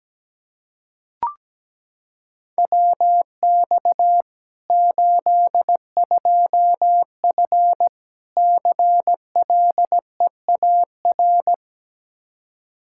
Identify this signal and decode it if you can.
Morse code, heard as a keyed tone, about 17 wpm, 700 Hz. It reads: WX 82F CLEAR
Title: Practice QSO - Exchange 2 - Info - 17wpm